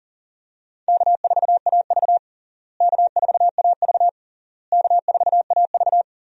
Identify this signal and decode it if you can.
Morse code, heard as a keyed tone, about 40 wpm, 700 Hz. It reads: K4AV K4AV K4AV
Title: USA Call Signs - 4 Characters - Repeated 3x - 40wpm